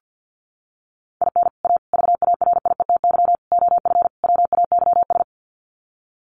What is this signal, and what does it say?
Morse code, heard as a keyed tone, about 50 wpm, 700 Hz. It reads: IN A VARIETY OF WAYS
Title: Sets of 5 Words Encore 50wpm